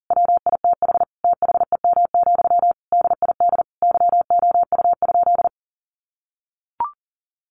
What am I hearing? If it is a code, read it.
Morse code, heard as a keyed tone, about 40 wpm, 700 Hz. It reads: WITH THEM, DID YOU?
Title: Sentences from Top 1500 Words 40wpm